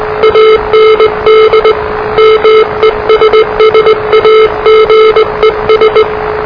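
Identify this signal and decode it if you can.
Morse code, heard as a keyed tone, about 20 wpm, 440 Hz, with noise. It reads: AND MESSAGES
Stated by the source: Morse challenge from NZ Net, the national amateur radio net of Aotearoa/New Zealand, which meets on 3535.0 kHz.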